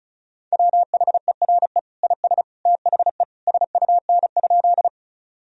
Morse code, heard as a keyed tone, about 36 wpm, 700 Hz. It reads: WHERE IS THE SUN?
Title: Sentences from Top 1000 Words - Review Part 1 - 35wpm